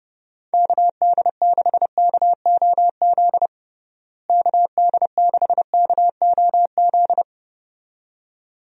Morse code, heard as a keyed tone, about 30 wpm, 700 Hz. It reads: KD6KOZ KD6KOZ
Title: USA Call Signs - 6 Characters - Repeated 2x - 30wpm